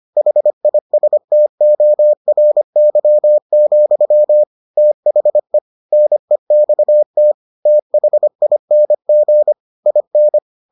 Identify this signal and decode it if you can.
Morse code, heard as a keyed tone, about 25 wpm, 600 Hz. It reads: HISTORY, THE NEXT THING IN